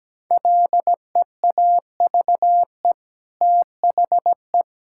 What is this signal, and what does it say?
Morse code, heard as a keyed tone, about 17 wpm, 700 Hz. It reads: LEAVE THE
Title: Sentences from Top 1000 Words - Review Part 5 - 17wpm